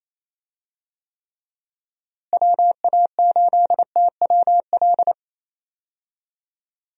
Morse code, heard as a keyed tone, about 28 wpm, 700 Hz. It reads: WA8TWL
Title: USA Call Signs - 6 Characters - Repeated 3x - 28wpm